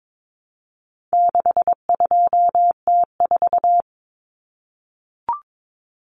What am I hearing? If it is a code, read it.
Morse code, heard as a keyed tone, about 22 wpm, 700 Hz. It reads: 62T4
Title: Common Cut Numbers - 4-Digit 22wpm